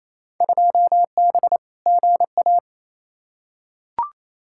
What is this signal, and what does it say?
Morse code, heard as a keyed tone, about 28 wpm, 700 Hz. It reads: 2B GA
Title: Field Day Exchanges Encore - 28wpm